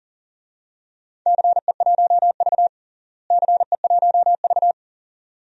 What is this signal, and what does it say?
Morse code, heard as a keyed tone, about 40 wpm, 700 Hz. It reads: CE1V CE1V
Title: DXCC-100 Call Signs - 4 Characters - Repeated 2x - 40wpm